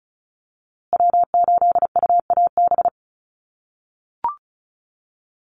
Morse code, heard as a keyed tone, about 34 wpm, 700 Hz. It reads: W8UAB
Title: USA Call Signs Repeated 2x 35wpm